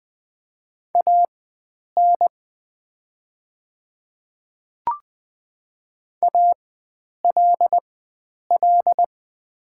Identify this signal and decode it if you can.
Morse code, heard as a keyed tone, about 20 wpm, spaced as for 9 wpm, 700 Hz. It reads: AN ALL
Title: 4x Character Spacing - Top 100 Words - 20wpm